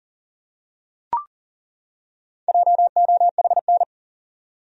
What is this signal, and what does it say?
Morse code, heard as a keyed tone, about 40 wpm, 700 Hz. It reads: JOHN
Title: USA Common Names - Max 4 Letters - 40wpm